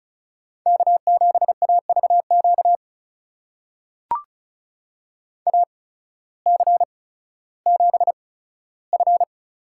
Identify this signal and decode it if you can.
Morse code, heard as a keyed tone, about 36 wpm, spaced as for 16 wpm, 700 Hz. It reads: K7AVQ AC7F
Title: USA Call Signs - 5 Characters - 8x Character Spacing - 35wpm